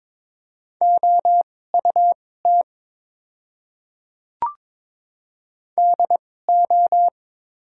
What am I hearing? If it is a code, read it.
Morse code, heard as a keyed tone, about 21 wpm, spaced as for 16 wpm, 690 Hz. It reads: OUT DO
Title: 2x Character Spacing - Top 100 Words - 22wpm